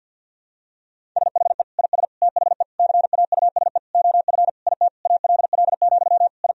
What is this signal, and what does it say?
Morse code, heard as a keyed tone, about 52 wpm, 700 Hz. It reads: SHE IS THE CAUSE OF IT ALL, I
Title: Sentences from Top 1000 Words - Review Part 1 - 50wpm